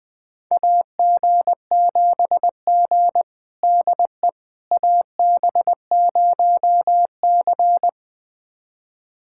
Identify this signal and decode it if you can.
Morse code, heard as a keyed tone, about 21 wpm, 700 Hz. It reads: AG7G DE AB0C
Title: USA Callsign DE Callsign - 4 Characters - 20wpm